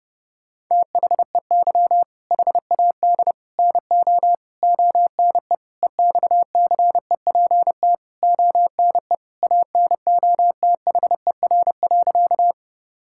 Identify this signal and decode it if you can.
Morse code, heard as a keyed tone, about 32 wpm, 700 Hz. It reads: THEY HAD NO ONE EXCEPT ONE ANOTHER.
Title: Sentences from Top 1600 Words 30wpm